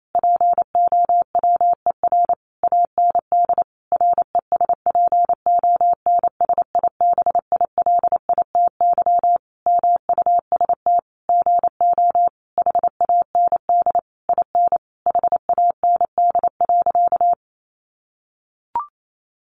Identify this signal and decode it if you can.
Morse code, heard as a keyed tone, about 28 wpm, 700 Hz. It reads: POWER AND RESPONSIBILITY MUST GO HAND IN HAND.
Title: Sentences from Top 1200 Words 28wpm